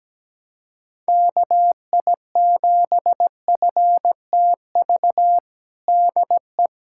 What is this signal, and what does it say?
Morse code, heard as a keyed tone, about 17 wpm, 700 Hz. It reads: KI7FTV DE
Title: USA Callsign DE Callsign - 6 Characters - 17wpm